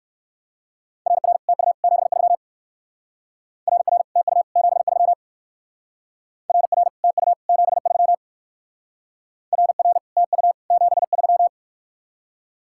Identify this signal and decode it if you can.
Morse code, heard as a keyed tone, about 55 wpm, 700 Hz. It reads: RR TU 73 RR TU 73 RR TU 73 RR TU 73
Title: Speed-Racing Common QSO Phrases - 45wpm